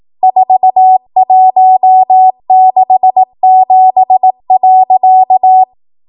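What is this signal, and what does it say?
Morse code, heard as a keyed tone, about 18 wpm, 750 Hz. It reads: 4167.
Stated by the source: ARRL 18 wpm Morse code practice archive transmission.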